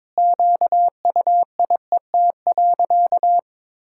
Morse code, heard as a keyed tone, about 22 wpm, 700 Hz. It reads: QUIET.
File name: practice / 22wpm / Sentences with 3-Word Phrases - 22wpm